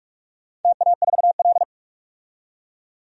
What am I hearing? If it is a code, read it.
Morse code, heard as a keyed tone, about 44 wpm, 700 Hz. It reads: TA4L
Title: DXCC-100 Call Signs - 4 Characters - 45wpm